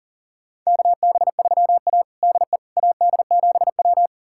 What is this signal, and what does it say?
Morse code, heard as a keyed tone, about 40 wpm, 700 Hz. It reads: KB3A DE AD7W
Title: USA Callsign DE Callsign - 4 Characters - 40wpm